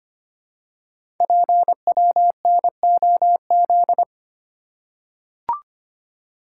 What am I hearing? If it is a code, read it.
Morse code, heard as a keyed tone, about 25 wpm, 700 Hz. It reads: PWNOZ
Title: Code Groups - 5 Letters - 25wpm